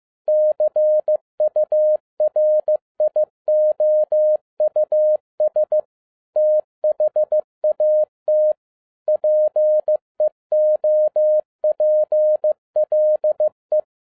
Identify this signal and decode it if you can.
Morse code, heard as a keyed tone, about 15 wpm, 610 Hz. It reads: CURIOUS THAT PEOPLE